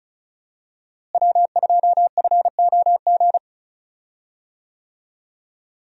Morse code, heard as a keyed tone, about 36 wpm, 700 Hz. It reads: W2FOG
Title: USA Call Signs - 5 Characters - Repeated 3x - 35wpm